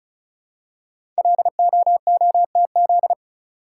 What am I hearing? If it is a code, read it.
Morse code, heard as a keyed tone, about 36 wpm, 700 Hz. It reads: LOOTZ
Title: Code Groups - 5 Letters - 35wpm